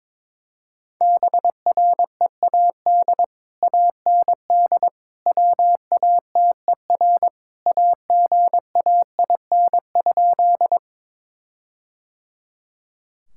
Morse code, heard as a keyed tone, about 22 wpm, 700 Hz. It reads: BREAD AND WATER AGAIN?
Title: Sentences from Top 1600 Words 22wpm